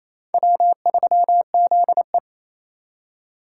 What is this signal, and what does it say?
Morse code, heard as a keyed tone, about 36 wpm, 700 Hz. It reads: W3ZE